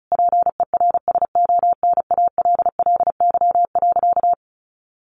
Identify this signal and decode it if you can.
Morse code, heard as a keyed tone, about 34 wpm, 700 Hz. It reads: PERSONALLY.
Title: Sentences from Top 1400 Words 35wpm